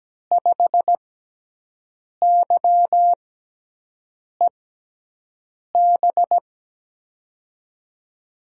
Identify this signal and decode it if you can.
Morse code, heard as a keyed tone, about 17 wpm, spaced as for 6 wpm, 700 Hz. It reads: 5YEB
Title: USA Call Signs - 6x Character Spacing - 17wpm